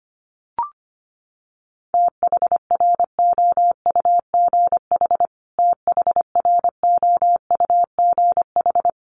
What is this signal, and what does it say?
Morse code, heard as a keyed tone, about 25 wpm, 700 Hz. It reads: THROUGH THROUGH
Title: Top 500 Words Repeated 25wpm